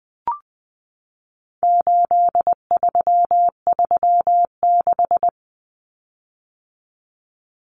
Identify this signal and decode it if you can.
Morse code, heard as a keyed tone, about 20 wpm, 700 Hz. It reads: 8336
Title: Common Cut Numbers - 4-Digit 20wpm